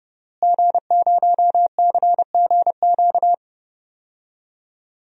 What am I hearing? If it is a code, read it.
Morse code, heard as a keyed tone, about 30 wpm, 700 Hz. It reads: G0CGQ
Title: DXCC-100 Call Signs - 5 Characters - Repeated 3x - 30wpm